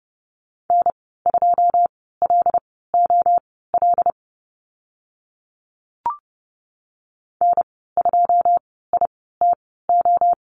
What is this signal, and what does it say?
Morse code, heard as a keyed tone, about 30 wpm, spaced as for 17 wpm, 700 Hz. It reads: N2LOL N2ITO
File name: Lesson 14g - Number 2 - Call Signs 30wpm